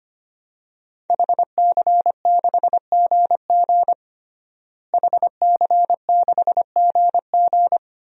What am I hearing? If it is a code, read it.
Morse code, heard as a keyed tone, about 25 wpm, 700 Hz. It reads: HC6GG HC6GG
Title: DXCC-100 Call Signs - 5 Characters - Repeated 2x - 25wpm